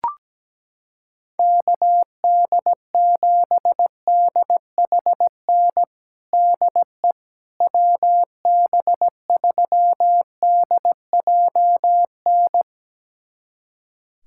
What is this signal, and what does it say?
Morse code, heard as a keyed tone, about 17 wpm, 700 Hz. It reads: KD7DHN DE WB3DJN
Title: USA Callsign DE Callsign 17wpm